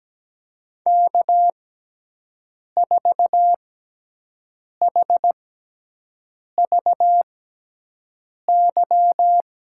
Morse code, heard as a keyed tone, about 17 wpm, spaced as for 6 wpm, 700 Hz. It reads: K4HVY
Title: USA Call Signs - 5 Characters - 6x Character Spacing - 17wpm